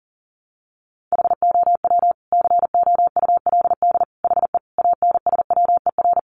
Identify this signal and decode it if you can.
Morse code, heard as a keyed tone, about 40 wpm, 700 Hz. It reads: HOW COULD HE ANSWER
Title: Sentences from Top 1300 Words 40wpm